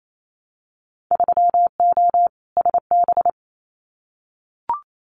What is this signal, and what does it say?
Morse code, heard as a keyed tone, about 28 wpm, 700 Hz. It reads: 3O SB
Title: Winter Field Day Exchanges Encore 28wpm